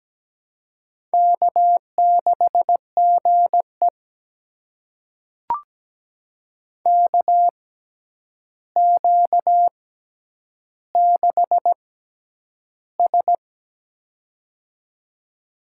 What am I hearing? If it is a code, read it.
Morse code, heard as a keyed tone, about 17 wpm, spaced as for 9 wpm, 700 Hz. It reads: K6GE KQ6S
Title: USA Call Signs - 6x Character Spacing - 17wpm